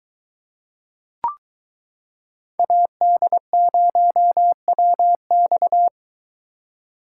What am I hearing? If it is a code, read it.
Morse code, heard as a keyed tone, about 23 wpm, 700 Hz. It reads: AD0WX